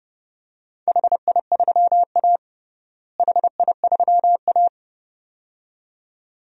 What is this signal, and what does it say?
Morse code, heard as a keyed tone, about 32 wpm, 700 Hz. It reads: HI3A HI3A
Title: DXCC-100 Call Signs - 4 Characters - Repeated 2x - 30wpm